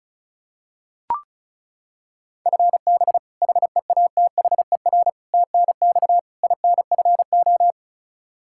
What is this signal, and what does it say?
Morse code, heard as a keyed tone, about 36 wpm, 700 Hz. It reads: FB HEATHER TNX INFO
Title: Practice QSO - Exchange 3 - More Info - 35wpm